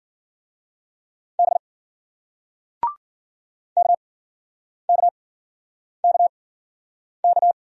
Morse code, heard as a keyed tone, about 56 wpm, spaced as for 9 wpm, 700 Hz. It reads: B KKKK